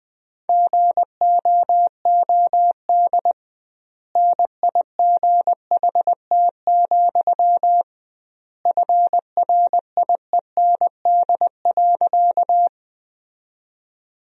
Morse code, heard as a keyed tone, about 21 wpm, 700 Hz. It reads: GOOD NIGHT, FRIEND.
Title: 2x Word-Spacing - Sentences from Top 300 Words - 20wpm